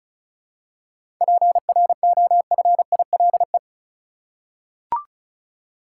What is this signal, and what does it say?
Morse code, heard as a keyed tone, about 36 wpm, 700 Hz. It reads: PROFILE
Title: Seven Letter Words - 35wpm